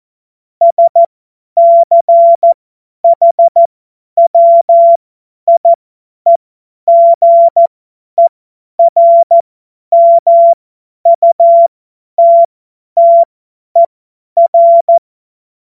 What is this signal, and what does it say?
Morse code, heard as a keyed tone, about 14 wpm, spaced as for 10 wpm, 680 Hz. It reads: SCHWIEGERMUTTER
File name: Morsekurs / Witze / Witz 18. Januar 2022, 50 F70